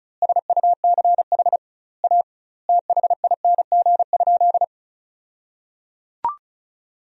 Morse code, heard as a keyed tone, about 36 wpm, 700 Hz. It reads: SUCH A THING?
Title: 2x Word-Spacing - Sentences from Top 200 Words - 35wpm